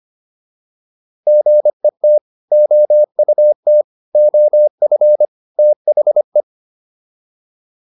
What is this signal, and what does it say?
Morse code, heard as a keyed tone, about 25 wpm, 600 Hz. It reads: GET OUT OF THE